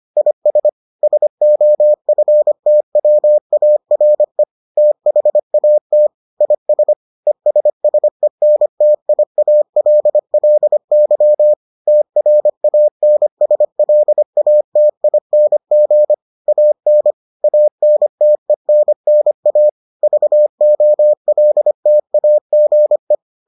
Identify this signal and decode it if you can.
Morse code, heard as a keyed tone, about 25 wpm, 600 Hz. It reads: IS SOFTWARE THAT IS ESSENTIALLY TRANSLATING AN ANTENNA VOLTAGE